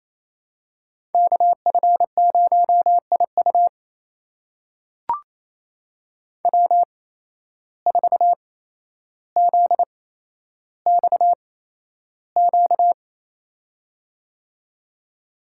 Morse code, heard as a keyed tone, about 28 wpm, spaced as for 12 wpm, 700 Hz. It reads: KF0IU W4ZXQ
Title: USA Call Signs - 5 Characters - 8x Character Spacing - 28wpm